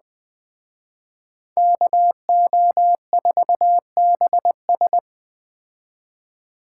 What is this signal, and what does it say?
Morse code, heard as a keyed tone, about 20 wpm, 700 Hz. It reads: KO4BS